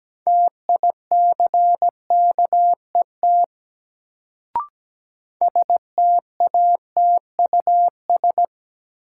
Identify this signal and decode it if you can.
Morse code, heard as a keyed tone, about 17 wpm, 700 Hz. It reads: TICKET STATUS
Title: Sentences from Top 1400 Words - New Word Drill - 17wpm